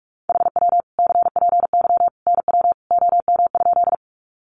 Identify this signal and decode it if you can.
Morse code, heard as a keyed tone, about 44 wpm, 700 Hz. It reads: HW CPY NW OM?
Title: Common QSO Phrases 45wpm